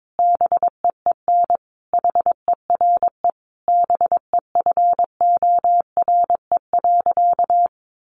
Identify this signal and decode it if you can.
Morse code, heard as a keyed tone, about 22 wpm, 700 Hz. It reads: BEEN HERE BEFORE.
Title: Additional Easy Phrases 22wpm